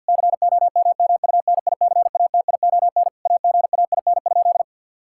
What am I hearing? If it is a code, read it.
Morse code, heard as a keyed tone, about 50 wpm, 700 Hz. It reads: COMMUNICATION AGAIN?
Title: Sentences from Top 1000 Words - Review Part 6 - 50wpm